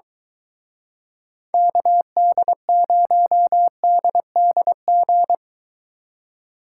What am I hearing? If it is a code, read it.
Morse code, heard as a keyed tone, about 23 wpm, 700 Hz. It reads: KD0DDG